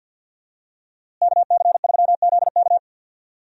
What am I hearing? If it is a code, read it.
Morse code, heard as a keyed tone, about 52 wpm, 700 Hz. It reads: KC3ZK
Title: USA Call Signs - 5 Characters - Repeated 2x - 50wpm